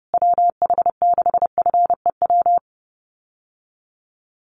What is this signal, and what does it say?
Morse code, heard as a keyed tone, about 30 wpm, 700 Hz. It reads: WH6FEW
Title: USA Call Signs Repeated 3x 30wpm